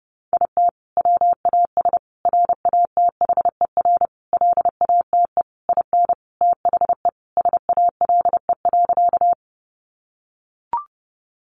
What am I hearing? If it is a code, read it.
Morse code, heard as a keyed tone, about 28 wpm, 700 Hz. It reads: IT WAS RATHER LATE IN THE SALE.
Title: Sentences from Top 1400 Words 30wpm